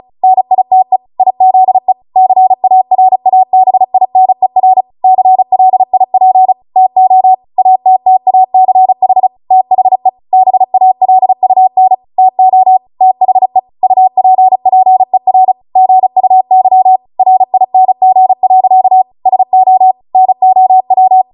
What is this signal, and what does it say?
Morse code, heard as a keyed tone, about 36 wpm, 750 Hz. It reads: NITE IZE CARABINER CLIP TO ATTACH THE BALUN TO THE UPPER GUY RING. SO NOW